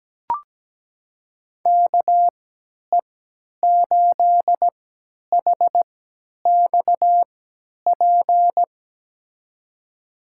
Morse code, heard as a keyed tone, about 17 wpm, spaced as for 10 wpm, 700 Hz. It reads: KE8HXP
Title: USA Call Signs - 3x Character Spacing - 17wpm